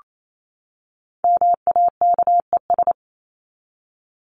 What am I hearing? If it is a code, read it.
Morse code, heard as a keyed tone, about 28 wpm, 700 Hz. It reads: MAKES
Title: ES:S Suffix - Top 100 Words with Base Words 28wpm